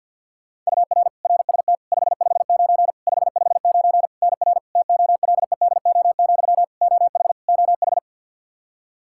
Morse code, heard as a keyed tone, about 50 wpm, 700 Hz. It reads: UR RST 559 559 NR TOLEDO, OH OH
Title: Practice QSO - Exchange 1 - Beginning - 50wpm